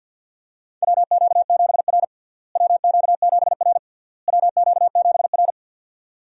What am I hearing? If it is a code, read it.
Morse code, heard as a keyed tone, about 46 wpm, 700 Hz. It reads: WQ7R WQ7R WQ7R
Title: USA Call Signs - 4 Characters - Repeated 3x - 50wpm